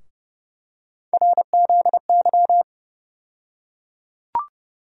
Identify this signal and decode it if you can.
Morse code, heard as a keyed tone, about 30 wpm, 700 Hz. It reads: RZY
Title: USA Call Sign Postfixes - 3 Letters 30wpm